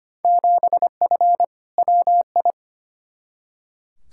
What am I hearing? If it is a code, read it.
Morse code, heard as a keyed tone, about 25 wpm, 700 Hz. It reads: 7F WI
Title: Field Day Exchanges Encore - 25wpm